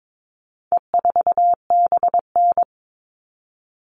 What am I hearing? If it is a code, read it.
Morse code, heard as a keyed tone, about 22 wpm, 700 Hz. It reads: E4BN
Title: All Cut Numbers - 4-Digit 22wpm